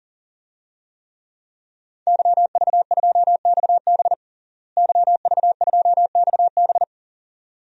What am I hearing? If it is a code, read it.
Morse code, heard as a keyed tone, about 39 wpm, 700 Hz. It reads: YV2XB YV2XB
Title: DXCC-100 Call Signs - 5 Characters - Repeated 2x - 40wpm